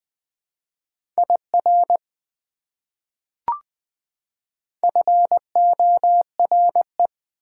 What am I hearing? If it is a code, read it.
Morse code, heard as a keyed tone, about 21 wpm, 700 Hz. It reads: IR FORE
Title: Prefixes and Suffixes 20wpm